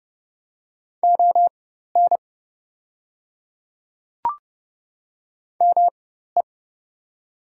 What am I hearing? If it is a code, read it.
Morse code, heard as a keyed tone, about 30 wpm, spaced as for 14 wpm, 700 Hz. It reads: ON ME
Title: 4x Character Spacing - Top 200 Words - 30wpm